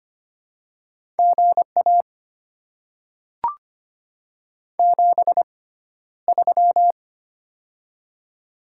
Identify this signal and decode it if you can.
Morse code, heard as a keyed tone, about 25 wpm, spaced as for 13 wpm, 700 Hz. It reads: GA 73
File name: Top 20 QSO Elements - 6x Character Spacing - 25wpm